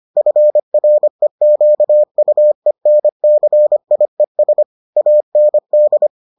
Morse code, heard as a keyed tone, about 25 wpm, 600 Hz. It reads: FREQUENCIES AND